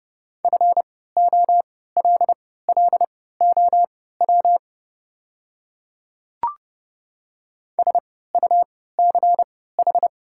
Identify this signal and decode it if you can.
Morse code, heard as a keyed tone, about 31 wpm, spaced as for 17 wpm, 700 Hz. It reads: FOLLOW SUCH
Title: 3x Character Spacing - Top 300 Words - 30wpm